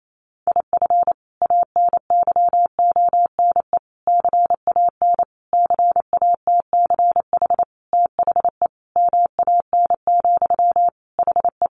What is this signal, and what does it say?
Morse code, heard as a keyed tone, about 28 wpm, 700 Hz. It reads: IF ANYONE CAN CATCH THE MAN, HE
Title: Sentences from Top 1500 Words 28wpm